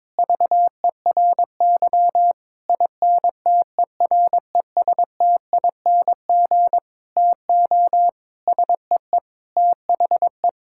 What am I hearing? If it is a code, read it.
Morse code, heard as a keyed tone, about 22 wpm, 700 Hz. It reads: VERY INTERESTING TO SEE THE
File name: Sentences with 7-Word Phrases - 22wpm